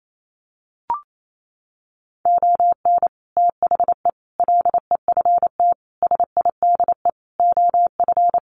Morse code, heard as a keyed tone, about 28 wpm, 700 Hz. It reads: ON THE LEFT SIDE OF
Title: Sets of 5 Words Encore2 28wpm